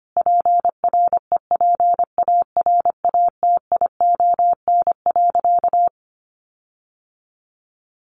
Morse code, heard as a keyed tone, about 25 wpm, 700 Hz. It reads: PREPARATION.
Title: Sentences from Top 1400 Words 25wpm